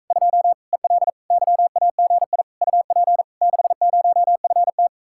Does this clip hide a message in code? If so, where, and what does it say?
Yes: Morse code from 0.1 to 4.9 s, heard as a keyed tone, about 45 wpm, 700 Hz. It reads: 2 EL YAGI UP 60FT